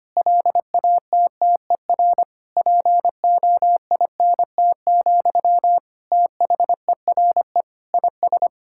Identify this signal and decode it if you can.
Morse code, heard as a keyed tone, about 25 wpm, 700 Hz. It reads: LATTER POINT, THERE IS
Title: Sentences from Top 1600 Words 25wpm